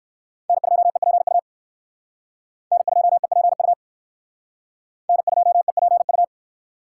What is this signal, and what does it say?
Morse code, heard as a keyed tone, about 58 wpm, 700 Hz. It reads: N2EPU N2EPU N2EPU